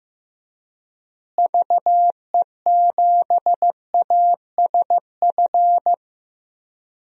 Morse code, heard as a keyed tone, about 15 wpm, 700 Hz. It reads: VE7ASF
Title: DXCC-100 Call Signs - 6 Characters - Repeated 3x - 15wpm